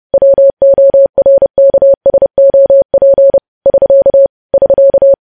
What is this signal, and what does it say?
Morse code, heard as a keyed tone, about 30 wpm, 550 Hz. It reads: WORKSOP <SK> <SK>
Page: news headlines in morse code at 30 wpm